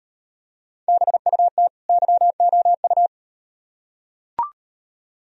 Morse code, heard as a keyed tone, about 38 wpm, 700 Hz. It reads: BUT YOU